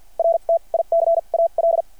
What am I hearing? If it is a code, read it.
Morse code, heard as a keyed tone, about 52 wpm, spaced as for 39 wpm, 660 Hz. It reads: RTICAL
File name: archived / CRSnet